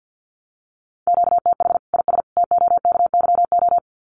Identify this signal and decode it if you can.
Morse code, heard as a keyed tone, about 50 wpm, 700 Hz. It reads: QTH IS TOKYO
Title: Common QSO Phrases 50wpm